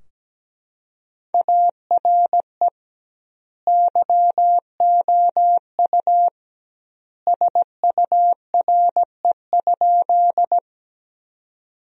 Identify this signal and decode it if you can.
Morse code, heard as a keyed tone, about 17 wpm, 700 Hz. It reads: ARE YOU SURE?
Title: 2x Word-Spacing - Sentences from Top 300 Words - 17wpm